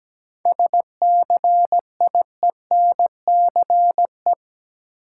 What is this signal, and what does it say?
Morse code, heard as a keyed tone, about 17 wpm, 700 Hz. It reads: SCIENCE
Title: Sentences from Top 1000 Words - Review Part 2 - New Word Drill - 17wpm